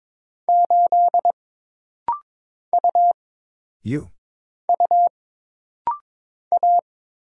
Morse code, heard as a keyed tone, about 22 wpm, 700 Hz. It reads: 8 U U A
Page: This rapid-fire timing is geared to push your ICR (Instant Character Recognition) skills.